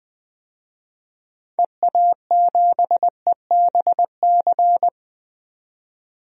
Morse code, heard as a keyed tone, about 20 wpm, 700 Hz. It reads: EA7EBC
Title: DXCC-100 Call Signs - 6 Characters - 20wpm